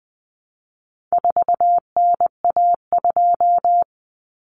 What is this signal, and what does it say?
Morse code, heard as a keyed tone, about 20 wpm, 700 Hz. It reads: 4NA2
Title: Common Cut Numbers - 4-Digit 20wpm